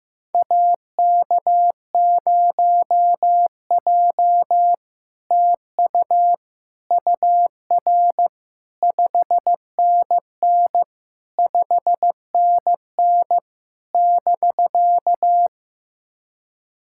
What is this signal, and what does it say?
Morse code, heard as a keyed tone, about 16 wpm, 700 Hz. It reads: AK0J TU UR 5NN 5NN <BK>
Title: SOTA QSO Practice 15wpm